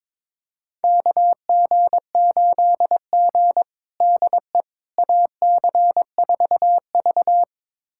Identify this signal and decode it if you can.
Morse code, heard as a keyed tone, about 22 wpm, 700 Hz. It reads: KG8G DE AC4V
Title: USA Callsign DE Callsign - 4 Characters - 22wpm